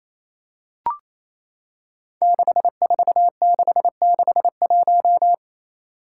Morse code, heard as a keyed tone, about 28 wpm, 700 Hz. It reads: 64661
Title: Five Challenging Letter-Number Combinations 28wpm